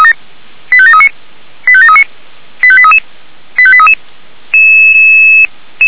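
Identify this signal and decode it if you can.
Five-tone selective call (ZVEI), heard as quick runs of five tones, tones of about 70 ms. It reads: 85739 85739 85833 85833
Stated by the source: ZVEI five-tone selective call received by SDR, archived by Feuerwehr Orlamünde fire brigade.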